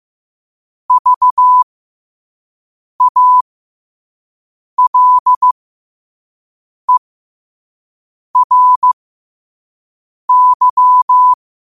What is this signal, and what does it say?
Morse code, heard as a keyed tone, about 15 wpm, spaced as for 5 wpm, 1000 Hz. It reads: VALERY
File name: 5wpm